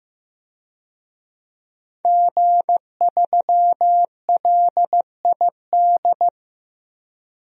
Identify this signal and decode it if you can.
Morse code, heard as a keyed tone, about 15 wpm, 700 Hz. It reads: G3LID
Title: DXCC-100 Call Signs - 5 Characters - 15wpm